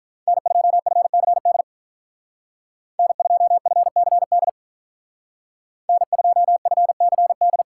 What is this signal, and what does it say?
Morse code, heard as a keyed tone, about 42 wpm, 700 Hz. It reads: N2FCD N2FCD N2FCD
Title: Speed Racing - USA 1x3 Call Signs 35wpm